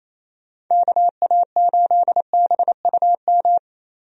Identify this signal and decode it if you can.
Morse code, heard as a keyed tone, about 28 wpm, 700 Hz. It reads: KA8BUM
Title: USA Call Signs - 6 Characters - Repeated 3x - 28wpm